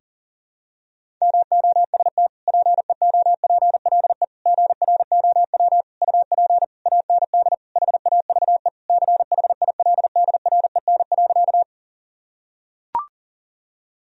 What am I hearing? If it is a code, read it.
Morse code, heard as a keyed tone, about 40 wpm, 700 Hz. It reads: MOST PEOPLE GROW UP AND HAVE CHILDREN.
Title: Sentences from Top 1000 Words - Review Part 1 - 40wpm